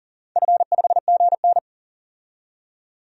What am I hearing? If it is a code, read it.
Morse code, heard as a keyed tone, about 40 wpm, 700 Hz. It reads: F5GN
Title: DXCC-100 Call Signs - 4 Characters - 40wpm